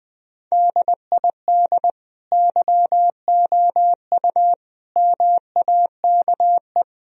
Morse code, heard as a keyed tone, about 20 wpm, 700 Hz. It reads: DID YOU MAKE
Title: Sentences from Top 1000 Words - Review Part 4 - 20wpm